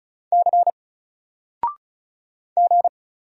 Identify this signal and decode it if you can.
Morse code, heard as a keyed tone, about 34 wpm, 700 Hz. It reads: C G
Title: Single Letter-Number 35wpm